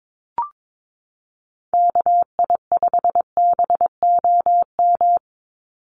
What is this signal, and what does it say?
Morse code, heard as a keyed tone, about 22 wpm, 700 Hz. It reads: KI5BOM
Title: USA Call Signs 22wpm